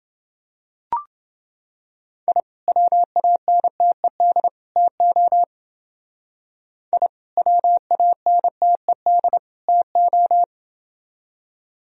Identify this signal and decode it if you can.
Morse code, heard as a keyed tone, about 28 wpm, 700 Hz. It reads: I WANTED TO I WANTED TO